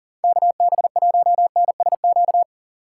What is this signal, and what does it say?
Morse code, heard as a keyed tone, about 40 wpm, 700 Hz. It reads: KB1NSQ
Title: USA Call Signs - 6 Characters - 3x Character Spacing - 40wpm